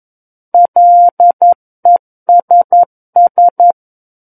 Morse code, heard as a keyed tone, about 11 wpm, 700 Hz. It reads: LESS
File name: Book One / 02-Chapter 2 - 11 WPM